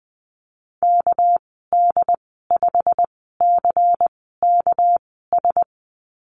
Morse code, heard as a keyed tone, about 20 wpm, spaced as for 14 wpm, 700 Hz. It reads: KD5CKS
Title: 219 Lesson 35g - Number 0 - Call Signs 20wpm